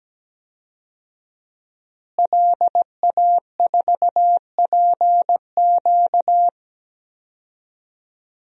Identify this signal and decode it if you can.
Morse code, heard as a keyed tone, about 17 wpm, 700 Hz. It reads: LA4PQ
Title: DXCC-100 Call Signs - 5 Characters - Repeated 2x - 17wpm